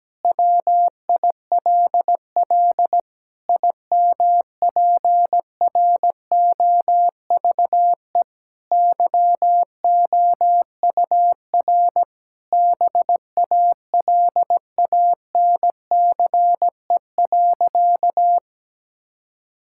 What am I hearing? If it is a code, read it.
Morse code, heard as a keyed tone, about 17 wpm, 700 Hz. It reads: WILL IMPROVE YOUR BALANCE.
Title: Sentences from Top 1000 Words - Review Part 6 - 17wpm